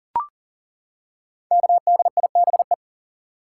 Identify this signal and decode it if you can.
Morse code, heard as a keyed tone, about 42 wpm, 700 Hz. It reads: KDIBE
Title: Code Groups - 5 Letters - 40wpm